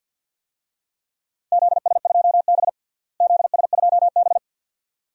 Morse code, heard as a keyed tone, about 52 wpm, 700 Hz. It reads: ZS2B ZS2B
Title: DXCC-100 Call Signs - 4 Characters - Repeated 2x - 50wpm